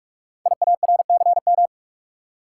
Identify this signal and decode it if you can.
Morse code, heard as a keyed tone, about 44 wpm, 700 Hz. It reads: IARCM
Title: Code Groups - 5 Letters - 45wpm